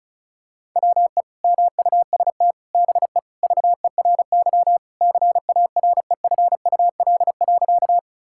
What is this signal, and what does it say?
Morse code, heard as a keyed tone, about 34 wpm, 700 Hz. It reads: WE MUST BE VERY CAREFUL.
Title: Sentences from Top 1600 Words 35wpm